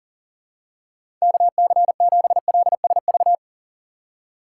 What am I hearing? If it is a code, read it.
Morse code, heard as a keyed tone, about 40 wpm, 700 Hz. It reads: KC7LSV
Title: USA Call Signs - 6 Characters - 3x Character Spacing - 40wpm